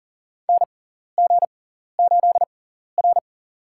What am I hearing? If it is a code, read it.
Morse code, heard as a keyed tone, about 40 wpm, spaced as for 14 wpm, 700 Hz. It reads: NG8R
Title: USA Call Signs - 4 Characters - 6x Character Spacing - 40wpm